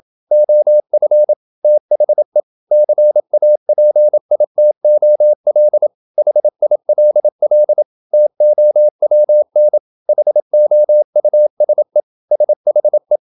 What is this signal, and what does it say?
Morse code, heard as a keyed tone, about 27 wpm, 600 Hz. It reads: OF THE CAPITOL HILL TOWN HOUSE SHE
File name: Never 0027wpm